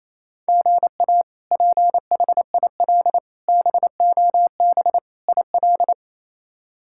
Morse code, heard as a keyed tone, about 28 wpm, 700 Hz. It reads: GA PHIL BOB IL
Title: K1USN SST QSO Practice - 28wpm